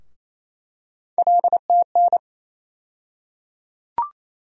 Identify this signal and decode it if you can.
Morse code, heard as a keyed tone, about 28 wpm, 700 Hz. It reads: LTN
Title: USA Call Sign Postfixes - 3 Letters 28wpm